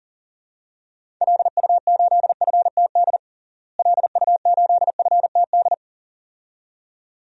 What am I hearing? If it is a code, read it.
Morse code, heard as a keyed tone, about 40 wpm, 700 Hz. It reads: LU8FTD LU8FTD
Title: DXCC-100 Call Signs - 6 Characters - Repeated 2x - 40wpm